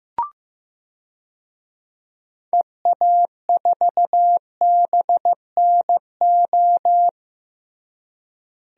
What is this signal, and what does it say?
Morse code, heard as a keyed tone, about 15 wpm, 700 Hz. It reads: EA4BNO
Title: DXCC-100 Call Signs - 6 Characters - Repeated 2x - 15wpm